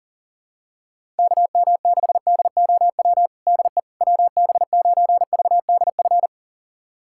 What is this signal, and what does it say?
Morse code, heard as a keyed tone, about 40 wpm, 700 Hz. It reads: KM6DOW DE WB9VDF
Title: USA Callsign DE Callsign - 6 Characters - 40wpm